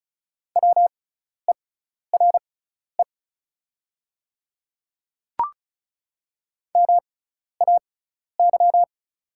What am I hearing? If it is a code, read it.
Morse code, heard as a keyed tone, about 36 wpm, spaced as for 12 wpm, 700 Hz. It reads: WERE MAY
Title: 6x Character Spacing - Top 500 Words - 35wpm